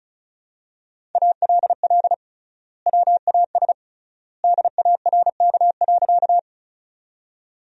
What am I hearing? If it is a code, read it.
Morse code, heard as a keyed tone, about 34 wpm, 700 Hz. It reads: ALL WAS DARK.
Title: 3x Word-Spacing - Sentences from Top 500 Words - 35wpm